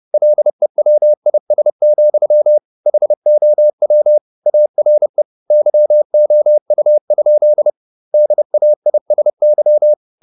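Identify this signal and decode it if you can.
Morse code, heard as a keyed tone, about 30 wpm, 600 Hz. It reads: LEWIS, HOW ARE YOU? DAISY
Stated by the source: Morse code at 30 wpm, the novel Never by Ken Follett in English, from CW med Gnister.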